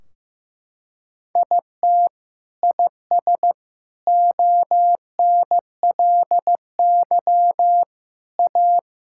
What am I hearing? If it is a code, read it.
Morse code, heard as a keyed tone, about 15 wpm, 700 Hz. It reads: IT IS ONLY A
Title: Sentences from Top 1600 Words 15wpm